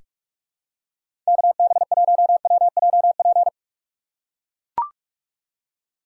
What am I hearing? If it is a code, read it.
Morse code, heard as a keyed tone, about 44 wpm, 700 Hz. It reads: KB1WJP
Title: USA Call Signs - 6 Characters - 2x Character Spacing - 45wpm